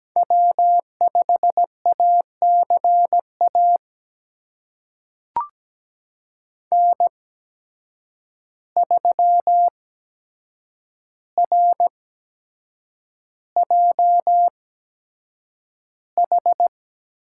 Morse code, heard as a keyed tone, about 17 wpm, spaced as for 7 wpm, 700 Hz. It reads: W5ACA N3RJH
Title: USA Call Signs - 8x Character Spacing - 17wpm